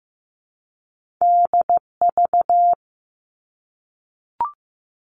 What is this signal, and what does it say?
Morse code, heard as a keyed tone, about 15 wpm, 700 Hz. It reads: DV
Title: Top Two Letter Combinations in Words 15wpm